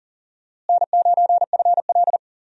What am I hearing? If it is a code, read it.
Morse code, heard as a keyed tone, about 40 wpm, 700 Hz. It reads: N9FL
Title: USA Call Signs - 4 Characters - Repeated 3x - 40wpm